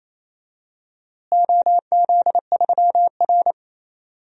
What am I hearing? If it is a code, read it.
Morse code, heard as a keyed tone, about 27 wpm, 700 Hz. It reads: OZ3R
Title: DXCC-100 Call Signs - 4 Characters - Repeated 2x - 28wpm